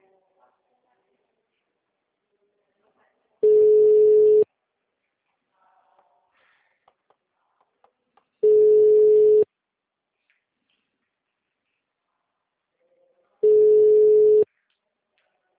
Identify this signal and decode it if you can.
Ringback tone (Europe, 425 Hz).